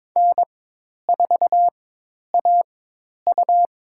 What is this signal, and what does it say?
Morse code, heard as a keyed tone, about 22 wpm, spaced as for 10 wpm, 700 Hz. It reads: N4AU
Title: USA Call Signs - 4x Character Spacing - 22wpm